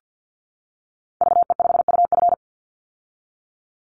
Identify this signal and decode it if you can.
Morse code, heard as a keyed tone, about 50 wpm, 700 Hz. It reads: VE5UF
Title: Call Signs - 50wpm